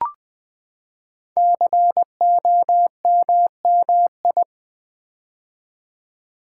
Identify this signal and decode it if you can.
Morse code, heard as a keyed tone, about 20 wpm, 700 Hz. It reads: COMMI
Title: Build A Word - 5 Letters At A Time - 10-Letter Words - 20wpm